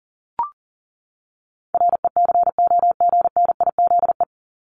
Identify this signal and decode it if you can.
Morse code, heard as a keyed tone, about 38 wpm, 700 Hz. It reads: RECOGNIZE
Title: Top 1251-1500 Words 40wpm